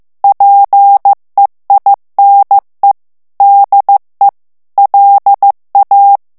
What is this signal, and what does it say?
Morse code, heard as a keyed tone, about 15 wpm, 800 Hz. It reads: PEINE DE LA
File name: morse mp3 / FRANCAIS TEXTE CLAIR 2 VITESSE 14 MOTS MINUTES